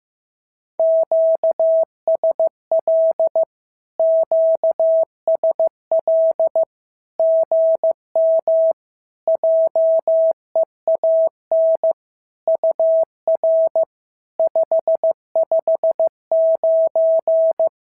Morse code, heard as a keyed tone, about 15 wpm, 650 Hz. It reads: QSL QSL GM JEAN UR 559
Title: SOTA QSO Practice 15wpm